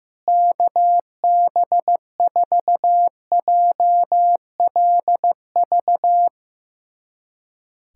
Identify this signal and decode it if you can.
Morse code, heard as a keyed tone, about 15 wpm, 700 Hz. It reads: KB4JLV
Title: USA Callsign DE Callsign - 6 Characters - 15wpm